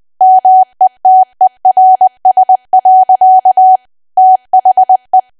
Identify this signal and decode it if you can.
Morse code, heard as a keyed tone, about 20 wpm, 750 Hz. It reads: METERS. THE
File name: cwa / W1AW-Level3 / 20WPM